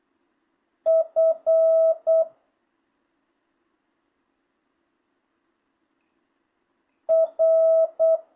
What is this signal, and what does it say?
Morse code, heard as a keyed tone, about 8 wpm, 650 Hz. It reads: F R
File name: jh8cbh / 2025cw